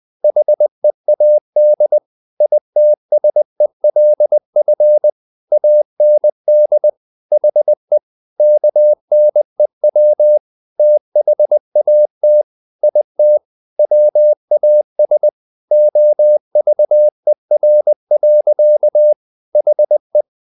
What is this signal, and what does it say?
Morse code, heard as a keyed tone, about 20 wpm, 600 Hz. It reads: HEAD ITSELF AND HE KNEW THAT IT WAS OVER. HE
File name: the old man and the sea part 2 20wpm